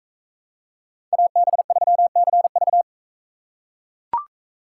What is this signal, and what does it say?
Morse code, heard as a keyed tone, about 40 wpm, 700 Hz. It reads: AB3CV